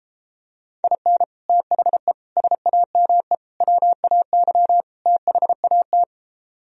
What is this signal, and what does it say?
Morse code, heard as a keyed tone, about 33 wpm, 700 Hz. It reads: IN THE SAME WAY THAT